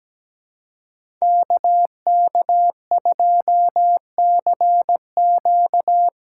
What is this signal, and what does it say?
Morse code, heard as a keyed tone, about 17 wpm, 700 Hz. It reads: KK2CQ
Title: USA Call Signs - 5 Characters - 3x Character Spacing - 17wpm